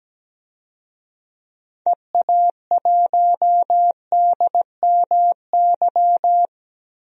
Morse code, heard as a keyed tone, about 17 wpm, 700 Hz. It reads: EA1DMY
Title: DXCC-100 Call Signs - 6 Characters - 17wpm